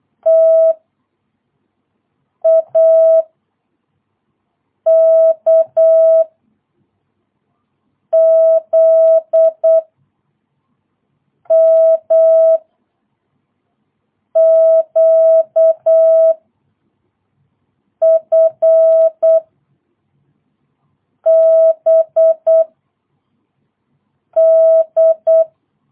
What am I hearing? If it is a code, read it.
Morse code, heard as a keyed tone, about 8 wpm, 650 Hz. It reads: T A K Z M Q F B D
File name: jh8cbh / 2022cw